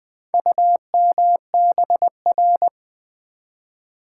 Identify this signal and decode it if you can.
Morse code, heard as a keyed tone, about 20 wpm, 700 Hz. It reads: UMBR
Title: Build A Word - 2 Letters At A Time - 8-Letter Words - 20wpm